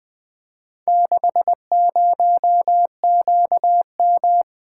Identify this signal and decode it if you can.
Morse code, heard as a keyed tone, about 20 wpm, 700 Hz. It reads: 60QM
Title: Four Letter-Number Combinations - 20wpm